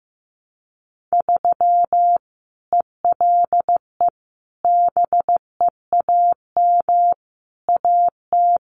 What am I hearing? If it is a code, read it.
Morse code, heard as a keyed tone, about 15 wpm, 700 Hz. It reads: 3 ELE BEAM AT
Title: Common QSO Phrases 15wpm